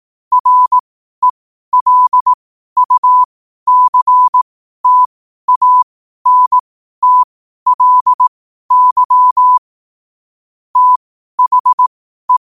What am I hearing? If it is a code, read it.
Morse code, heard as a keyed tone, about 18 wpm, spaced as for 12 wpm, 1000 Hz. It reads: RELUCTANTLY THE
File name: qotdfiles / 12wpm